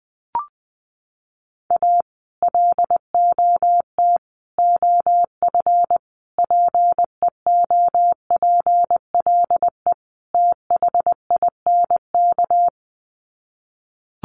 Morse code, heard as a keyed tone, about 20 wpm, 700 Hz. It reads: A LOT OF PEOPLE THINK
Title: Sets of 5 Words Encore 20wpm